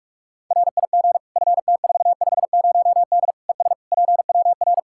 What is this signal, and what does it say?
Morse code, heard as a keyed tone, about 41 wpm, 700 Hz. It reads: RIG FT450D ES PWR
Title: Practice QSO - Exchange 2 - Info - 45wpm